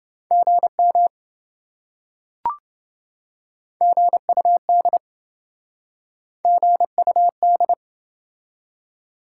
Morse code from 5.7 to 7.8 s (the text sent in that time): GUD